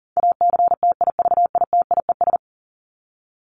Morse code, heard as a keyed tone, about 40 wpm, 700 Hz. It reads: ACTIVITIES
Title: ES:S Suffix - Top 100 Words with Base Words 40wpm